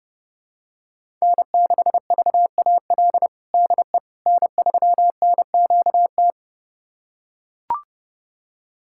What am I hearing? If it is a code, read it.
Morse code, heard as a keyed tone, about 30 wpm, 700 Hz. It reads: N6VAL DE N3NQT
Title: USA Callsign DE Callsign - 5 Characters - 30wpm